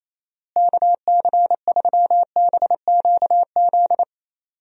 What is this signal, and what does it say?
Morse code, heard as a keyed tone, about 28 wpm, 700 Hz. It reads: KC3BQZ
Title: USA Call Signs - 6 Characters - Repeated 2x - 28wpm